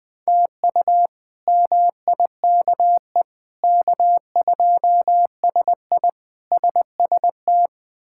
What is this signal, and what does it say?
Morse code, heard as a keyed tone, about 20 wpm, 700 Hz. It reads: TU MIKE K2SI SST
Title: K1USN SST QSO Practice - 20wpm